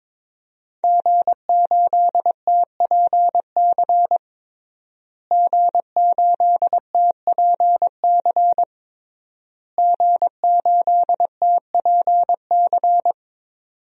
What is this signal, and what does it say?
Morse code, heard as a keyed tone, about 22 wpm, 700 Hz. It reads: G8TPC G8TPC G8TPC
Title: DXCC-100 Call Signs - 5 Characters - Repeated 3x - 22wpm